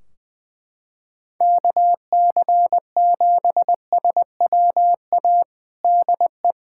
Morse code, heard as a keyed tone, about 20 wpm, 700 Hz. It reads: KC7SWA DE
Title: Practice QSO - Exchange 4 - Ending - 20wpm